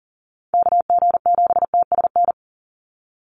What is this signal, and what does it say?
Morse code, heard as a keyed tone, about 40 wpm, 700 Hz. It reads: KG7TSN
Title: USA Call Signs 40wpm